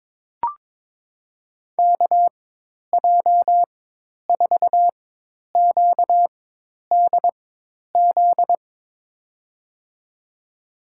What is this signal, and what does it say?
Morse code, heard as a keyed tone, about 21 wpm, spaced as for 10 wpm, 700 Hz. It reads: KJ4QDZ
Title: USA Call Signs - 4x Character Spacing - 22wpm